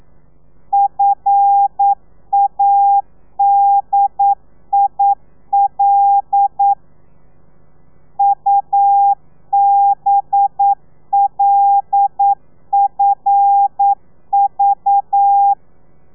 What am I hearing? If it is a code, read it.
Morse code, heard as a keyed tone, about 9 wpm, 800 Hz. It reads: FADIL UBLFV
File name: telegrafia / i2tfj / Lez5 40